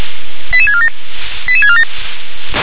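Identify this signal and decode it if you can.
Five-tone selective call (ZVEI), heard as quick runs of five tones, tones of about 70 ms. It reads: 88647 88647
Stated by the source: ZVEI five-tone selective call, received by SDR.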